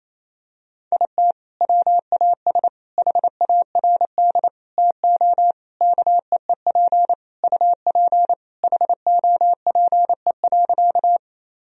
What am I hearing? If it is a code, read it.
Morse code, heard as a keyed tone, about 28 wpm, 700 Hz. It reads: IT WAS HARD TO KEEP UP HOPE.
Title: Sentences from Top 1000 Words - Review Part 1 - 28wpm